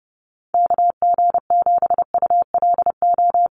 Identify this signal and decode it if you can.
Morse code, heard as a keyed tone, about 30 wpm, 700 Hz. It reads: KG7ULO
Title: USA Call Signs Repeated 3x 30wpm